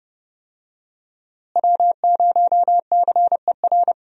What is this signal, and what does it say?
Morse code, heard as a keyed tone, about 30 wpm, 700 Hz. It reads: W0CER
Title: USA Call Signs - 5 Characters - Repeated 3x - 30wpm